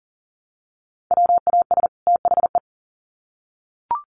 Morse code, heard as a keyed tone, about 40 wpm, 700 Hz. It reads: WAS THE
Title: Sets of 2 Words 40wpm